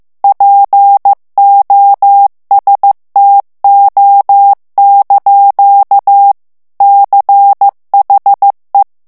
Morse code, heard as a keyed tone, about 15 wpm, 800 Hz. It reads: POSTO) CHE
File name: morse mp3 / ITALIEN texte clair 2 vitesse 14 mots minute